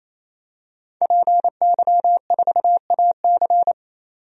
Morse code, heard as a keyed tone, about 28 wpm, 700 Hz. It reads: PY4AC
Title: DXCC-100 Call Signs - 5 Characters - 28wpm